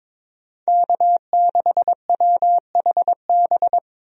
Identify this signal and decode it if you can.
Morse code, heard as a keyed tone, about 22 wpm, 700 Hz. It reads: K6WHB
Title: USA Call Signs - 5 Characters - Repeated 2x - 22wpm